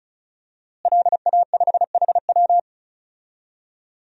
Morse code, heard as a keyed tone, about 36 wpm, 700 Hz. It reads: LA5HW
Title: DXCC-100 Call Signs - 5 Characters - Repeated 3x - 35wpm